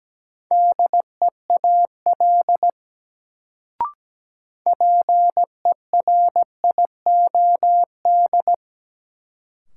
Morse code, heard as a keyed tone, about 17 wpm, 700 Hz. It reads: DEAL PERIOD
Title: Sentences from Top 800 Words - New Word Drill - 17wpm